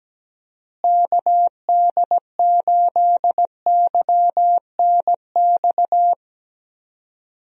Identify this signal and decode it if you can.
Morse code, heard as a keyed tone, about 17 wpm, 700 Hz. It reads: KD8YNX
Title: USA Call Signs - 6 Characters - Repeated 2x - 17wpm